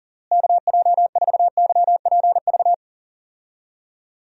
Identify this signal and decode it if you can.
Morse code, heard as a keyed tone, about 37 wpm, 700 Hz. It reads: KJ4YPV
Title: USA Call Signs - 6 Characters - Repeated 3x - 40wpm